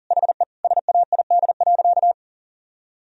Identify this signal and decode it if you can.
Morse code, heard as a keyed tone, about 40 wpm, 700 Hz. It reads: HE SAID.
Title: Sentences from Top 1000 Words - Review Part 4 - 40wpm